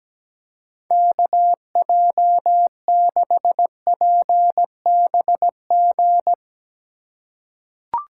Morse code, heard as a keyed tone, about 17 wpm, 700 Hz. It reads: KJ6PBG
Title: USA Call Signs - 6 Characters - 2x Character Spacing - 17wpm